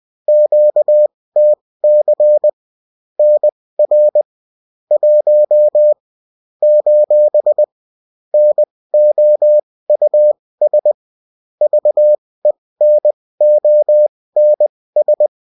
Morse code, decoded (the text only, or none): QTC NR 1 : NOUS VENONS